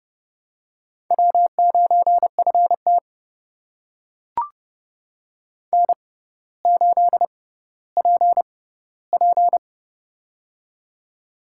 Morse code, heard as a keyed tone, about 30 wpm, spaced as for 15 wpm, 700 Hz. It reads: W9FT N8PP
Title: USA Call Signs - 4 Characters - 6x Character Spacing - 30wpm